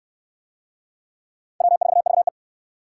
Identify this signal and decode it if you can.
Morse code, heard as a keyed tone, about 60 wpm, 700 Hz. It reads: W4VE